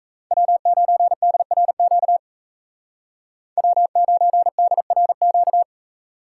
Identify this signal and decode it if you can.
Morse code, heard as a keyed tone, about 33 wpm, 700 Hz. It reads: W9DRQ W9DRQ
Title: Speed Racing - USA 1x3 Call Signs 28wpm